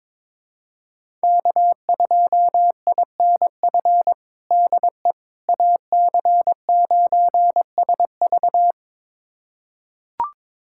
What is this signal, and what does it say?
Morse code, heard as a keyed tone, about 22 wpm, 700 Hz. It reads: K2INF DE AC9SV
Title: USA Callsign DE Callsign - 5 Characters - 22wpm